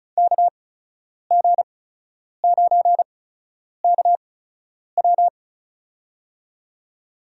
Morse code, heard as a keyed tone, about 34 wpm, spaced as for 10 wpm, 700 Hz. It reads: KG9KW
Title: USA Call Signs - 5 Characters - 8x Character Spacing - 35wpm